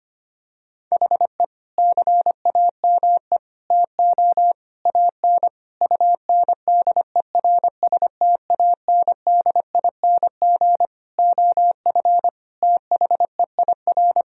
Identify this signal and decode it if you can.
Morse code, heard as a keyed tone, about 25 wpm, 700 Hz. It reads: HE CAME TO AN UNDERSTANDING OF THEIR
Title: Sentences from Top 1000 Words - Review Part 3 - 25wpm